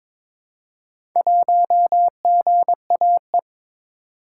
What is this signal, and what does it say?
Morse code, heard as a keyed tone, about 22 wpm, 700 Hz. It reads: 1GAE
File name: practice / 22wpm / Four Letter-Number Combinations - 22wpm